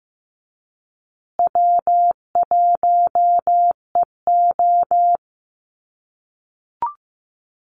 Morse code, heard as a keyed tone, about 15 wpm, 700 Hz. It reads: W1EO
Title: USA Call Signs Repeated 3x 15wpm